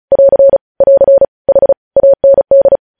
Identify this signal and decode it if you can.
Morse code, heard as a keyed tone, about 36 wpm, 550 Hz. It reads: <AR> <AR> H AND
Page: News headlines in Morse code at 35 wpm